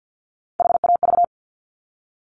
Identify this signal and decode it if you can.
Morse code, heard as a keyed tone, about 50 wpm, 700 Hz. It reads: HAV
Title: Top Three Letter Combinations in Words 50wpm